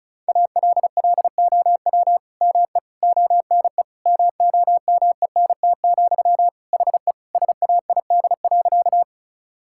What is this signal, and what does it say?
Morse code, heard as a keyed tone, about 36 wpm, 700 Hz. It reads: ALLOW ME ONE MOMENT, HE SAID.
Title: Sentences from Top 1000 Words - Review Part 5 - 35wpm